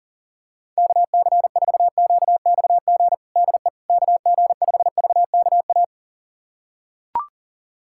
Morse code, heard as a keyed tone, about 40 wpm, 700 Hz. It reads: KC4QXG DE KG5VKA
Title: USA Callsign DE Callsign - 6 Characters - 40wpm